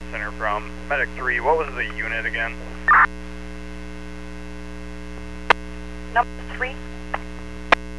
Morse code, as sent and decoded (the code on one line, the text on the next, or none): none